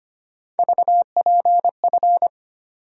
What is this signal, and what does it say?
Morse code, heard as a keyed tone, about 25 wpm, 700 Hz. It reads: VPF
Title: USA Call Sign Postfixes - 3 Letters 25wpm